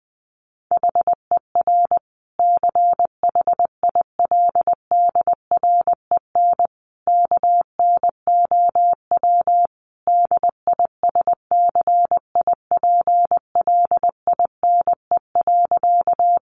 Morse code, heard as a keyed tone, about 20 wpm, 700 Hz. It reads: HER CHILDREN KNOW DISCIPLINE.
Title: Sentences from Top 900 Words 20wpm